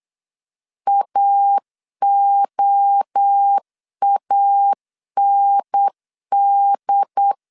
Morse code, heard as a keyed tone, about 8 wpm, 800 Hz. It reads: AOAND
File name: CW / DET / 1 3 1500C S B 8wpm 4141